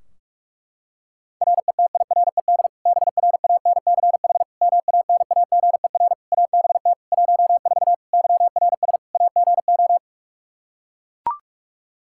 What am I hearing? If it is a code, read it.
Morse code, heard as a keyed tone, about 41 wpm, 700 Hz. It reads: RETIRED BRANCH MANAGER ABT 14 YRS AGO
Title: Practice QSO - Exchange 3 - More Info - 45wpm